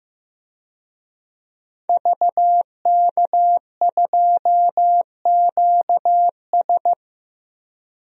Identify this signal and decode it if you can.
Morse code, heard as a keyed tone, about 15 wpm, 700 Hz. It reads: VK2QS
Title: DXCC-100 Call Signs - 5 Characters - Repeated 2x - 15wpm